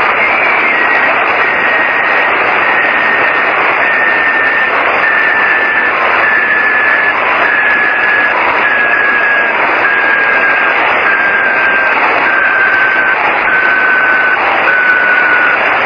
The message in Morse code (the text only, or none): none